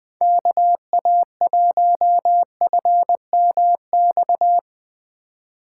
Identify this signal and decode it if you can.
Morse code, heard as a keyed tone, about 20 wpm, 700 Hz. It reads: KA1FMX
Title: Speed Racing - USA 2x3 Call Signs 17wpm